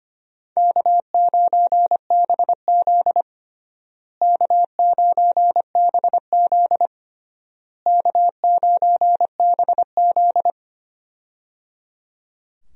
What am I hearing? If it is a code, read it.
Morse code, heard as a keyed tone, about 25 wpm, 700 Hz. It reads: K9BZ K9BZ K9BZ
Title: USA Call Signs - 4 Characters - Repeated 3x - 25wpm